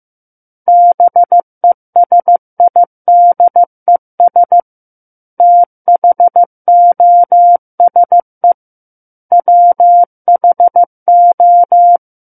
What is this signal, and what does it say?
Morse code, heard as a keyed tone, about 15 wpm, 700 Hz. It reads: BESIDES THOSE WHO
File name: Book One / 06-Chapter 6 - 15 WPM